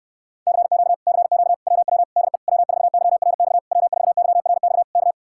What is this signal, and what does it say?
Morse code, heard as a keyed tone, about 65 wpm, 700 Hz. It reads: CQ CQ WW DE W4YRX W4YRX K